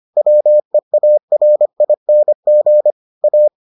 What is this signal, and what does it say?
Morse code, heard as a keyed tone, about 25 wpm, 600 Hz. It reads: WEARING A